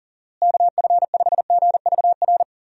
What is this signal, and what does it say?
Morse code, heard as a keyed tone, about 40 wpm, 700 Hz. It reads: KF5GVR
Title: USA Call Signs - 6 Characters - Repeated 3x - 40wpm